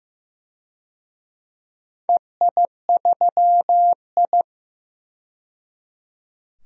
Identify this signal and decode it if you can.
Morse code, heard as a keyed tone, about 15 wpm, 700 Hz. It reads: EI3I
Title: DXCC-100 Call Signs - 4 Characters - 15wpm